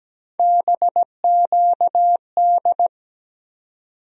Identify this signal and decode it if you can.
Morse code, heard as a keyed tone, about 17 wpm, 700 Hz. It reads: BQD
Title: USA Call Sign Postfixes - 3 Letters 17wpm